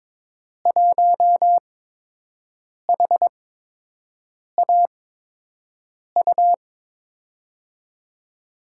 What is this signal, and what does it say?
Morse code, heard as a keyed tone, about 22 wpm, spaced as for 6 wpm, 700 Hz. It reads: 1HAU
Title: USA Call Signs - 5 Characters - 8x Character Spacing - 22wpm